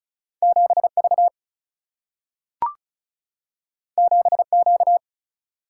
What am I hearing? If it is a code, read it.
Morse code, heard as a keyed tone, about 36 wpm, 700 Hz. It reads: 7 V 7 Q
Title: Single-Letter Single-Number 35wpm